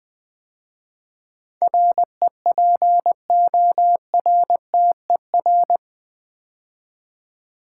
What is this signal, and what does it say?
Morse code, heard as a keyed tone, about 20 wpm, 700 Hz. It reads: REPORTER
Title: 8 Letter Words - First 3 Letters Given - 20wpm